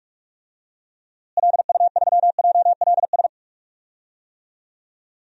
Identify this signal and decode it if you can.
Morse code, heard as a keyed tone, about 46 wpm, 700 Hz. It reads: LU3JLS
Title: DXCC-100 Call Signs - 6 Characters - Repeated 3x - 45wpm